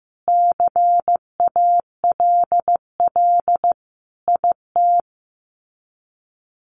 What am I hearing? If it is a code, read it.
Morse code, heard as a keyed tone, about 14 wpm, 700 Hz. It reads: CALL IT
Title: Sets of 5 Words Encore2 15wpm